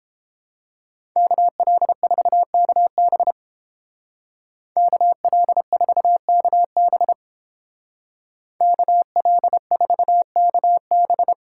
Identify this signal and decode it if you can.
Morse code, heard as a keyed tone, about 31 wpm, 700 Hz. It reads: KL4KB KL4KB KL4KB